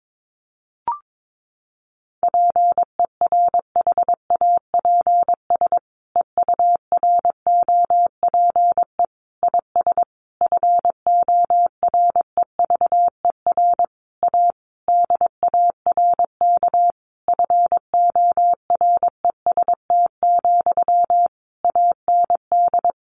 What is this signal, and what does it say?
Morse code, heard as a keyed tone, about 22 wpm, 700 Hz. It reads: PERHAPS EUROPE IS FOREVER A DARK FOREST, AND
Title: A Sound Of Thunder 22wpm